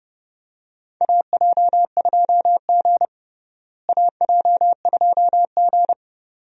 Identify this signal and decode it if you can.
Morse code, heard as a keyed tone, about 30 wpm, 700 Hz. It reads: AJ2G AJ2G
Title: USA Call Signs - 4 Characters - Repeated 2x - 30wpm